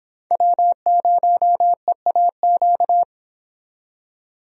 Morse code, heard as a keyed tone, about 26 wpm, 700 Hz. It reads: W0EAQ
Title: Speed Racing - USA 1x3 Call Signs 22wpm